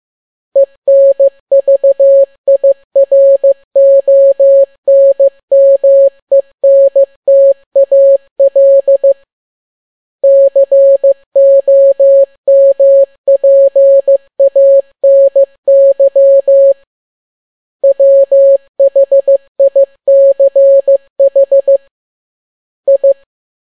Morse code, heard as a keyed tone, about 15 wpm, 550 Hz. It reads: ENVIRONMENTAL COMPANY WHICH I